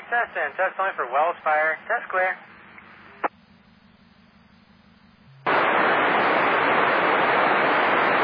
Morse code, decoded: none

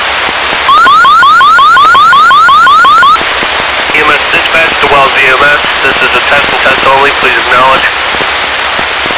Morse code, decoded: none

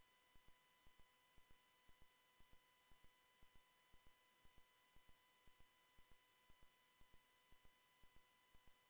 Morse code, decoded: none

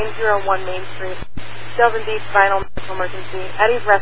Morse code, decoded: none

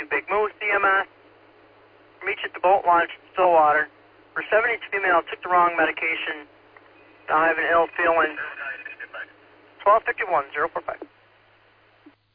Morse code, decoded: none

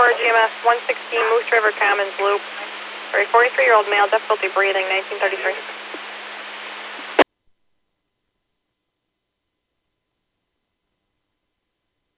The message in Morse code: none

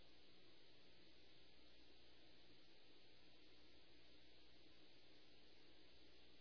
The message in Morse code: none

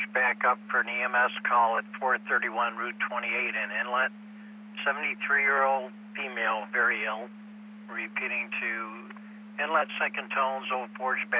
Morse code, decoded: none

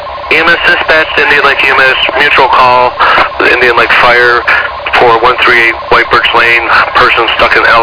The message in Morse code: none

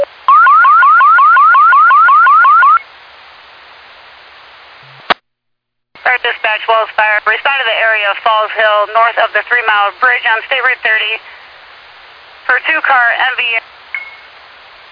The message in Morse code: none

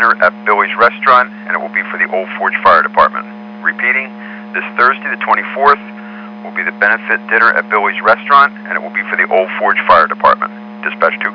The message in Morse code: none

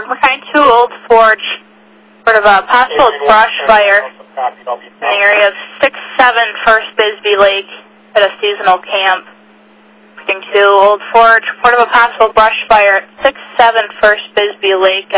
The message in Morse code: none

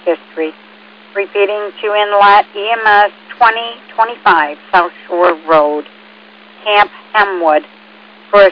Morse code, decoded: none

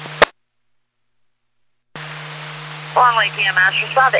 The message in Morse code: none

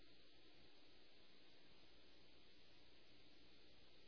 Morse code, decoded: none

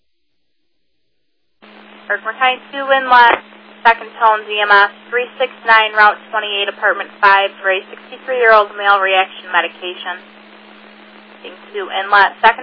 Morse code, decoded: none

